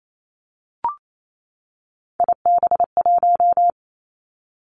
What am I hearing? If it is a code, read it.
Morse code, heard as a keyed tone, about 28 wpm, 690 Hz. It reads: IB1
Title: Three Challenging Letter-Number Combinations 28wpm